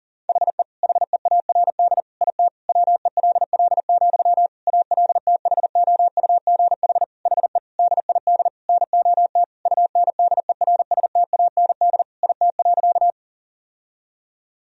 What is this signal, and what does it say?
Morse code, heard as a keyed tone, about 40 wpm, 700 Hz. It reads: HE HEARD IT WELL, ALTHOUGH HE DID NOT UNDERSTAND IT.
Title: Sentences from Top 1000 Words - Review Part 3 - 40wpm